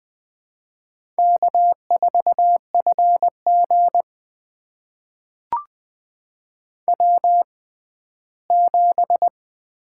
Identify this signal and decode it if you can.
Morse code, heard as a keyed tone, about 21 wpm, spaced as for 14 wpm, 700 Hz. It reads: K4FG W7
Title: USA Call Signs - 4 Characters - 6x Character Spacing - 20wpm